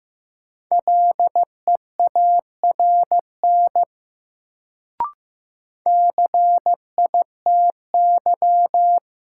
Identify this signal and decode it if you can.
Morse code, heard as a keyed tone, about 16 wpm, 700 Hz. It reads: LEARN CITY
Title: Sentences from Top 300 Words - New Word Drill - 15wpm